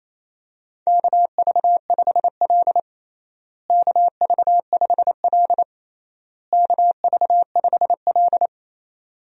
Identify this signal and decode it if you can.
Morse code, heard as a keyed tone, about 28 wpm, 700 Hz. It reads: KV5L KV5L KV5L
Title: USA Call Signs - 4 Characters - Repeated 3x - 28wpm